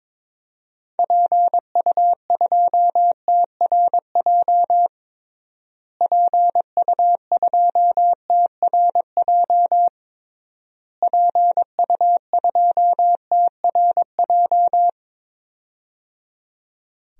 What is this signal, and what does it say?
Morse code, heard as a keyed tone, about 21 wpm, 700 Hz. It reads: PU2TRJ PU2TRJ PU2TRJ
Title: DXCC-100 Call Signs - 6 Characters - Repeated 3x - 22wpm